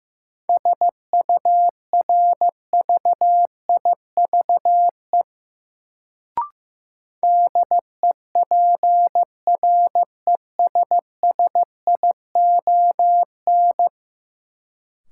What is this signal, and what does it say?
Morse code, heard as a keyed tone, about 16 wpm, 700 Hz. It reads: SURVIVE DEPRESSION
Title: Sentences from Top 1300 Words - New Word Drill - 15wpm